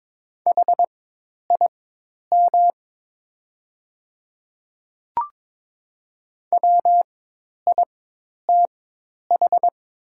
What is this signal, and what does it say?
Morse code, heard as a keyed tone, about 23 wpm, spaced as for 10 wpm, 700 Hz. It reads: HIM WITH
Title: 4x Character Spacing - Top 100 Words - 22wpm